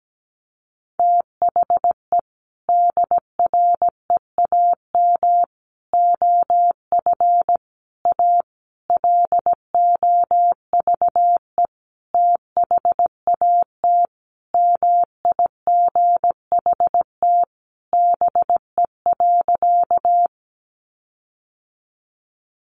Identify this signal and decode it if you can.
Morse code, heard as a keyed tone, about 17 wpm, 700 Hz. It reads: THE DREAM OF A LOVE THAT MIGHT BE.
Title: Sentences from Top 1500 Words 17wpm